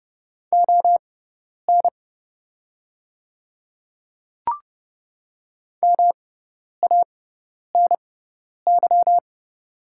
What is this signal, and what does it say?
Morse code, heard as a keyed tone, about 29 wpm, spaced as for 10 wpm, 700 Hz. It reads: ON MANY
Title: 6x Character Spacing - Top 300 Words - 30wpm